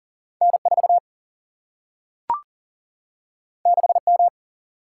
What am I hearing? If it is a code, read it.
Morse code, heard as a keyed tone, about 39 wpm, 700 Hz. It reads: N 4 6 M
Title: Single-Letter Single-Number 40wpm